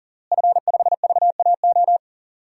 Morse code, heard as a keyed tone, about 39 wpm, 700 Hz. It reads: F5VAO
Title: DXCC-100 Call Signs - 5 Characters - Repeated 3x - 40wpm